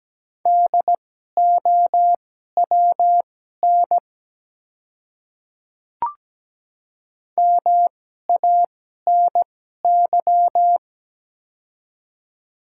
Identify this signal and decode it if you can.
Morse code, heard as a keyed tone, about 17 wpm, spaced as for 12 wpm, 700 Hz. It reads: DOWN MANY
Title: 2x Character Spacing - Top 100 Words - 17wpm